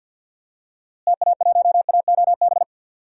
Morse code, heard as a keyed tone, about 46 wpm, 700 Hz. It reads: TA1AOB
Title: DXCC-100 Call Signs - 6 Characters - Repeated 2x - 50wpm